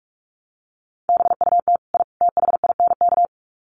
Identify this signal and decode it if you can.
Morse code, heard as a keyed tone, about 44 wpm, 700 Hz. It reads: BUT I THINK
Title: Sets of 3 Words 45wpm